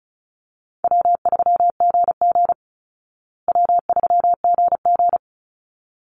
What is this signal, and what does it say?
Morse code, heard as a keyed tone, about 34 wpm, 700 Hz. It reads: W3GG W3GG
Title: Easy Call Signs Repeated 2x - 35wpm